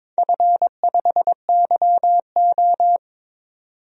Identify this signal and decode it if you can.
Morse code, heard as a keyed tone, about 22 wpm, 700 Hz. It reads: F5YO
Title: DXCC-100 Call Signs - 4 Characters - 22wpm